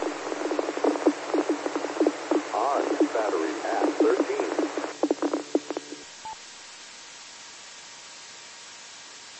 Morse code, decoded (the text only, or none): none